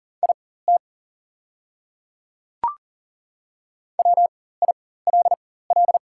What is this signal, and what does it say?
Morse code, heard as a keyed tone, about 40 wpm, spaced as for 19 wpm, 690 Hz. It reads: IT WILL
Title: 4x Character Spacing - Top 100 Words - 40wpm